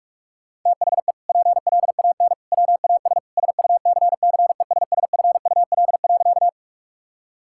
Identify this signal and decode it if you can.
Morse code, heard as a keyed tone, about 47 wpm, 700 Hz. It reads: THE PLAN WAS SUCCESSFUL.
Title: Sentences from Top 1000 Words - Review Part 6 - 45wpm